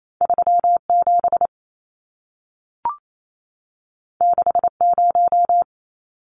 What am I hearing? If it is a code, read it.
Morse code, heard as a keyed tone, about 28 wpm, 700 Hz. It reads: 37 60
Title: Numbers - 2-Digit 28wpm